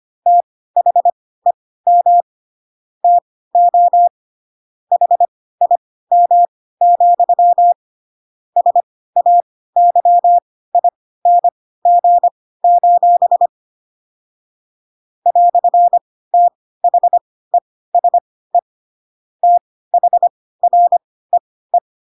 Morse code, heard as a keyed tone, about 25 wpm, spaced as for 16 wpm, 700 Hz. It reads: THEM TO HIM, SAYING: "THESE THREE
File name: Pinocchio / Pinocchio ch7 10 25 ik1hge